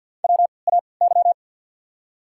Morse code, heard as a keyed tone, about 54 wpm, spaced as for 29 wpm, 700 Hz. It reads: WAY
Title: 3x Character Spacing - Top 300 Words - 50wpm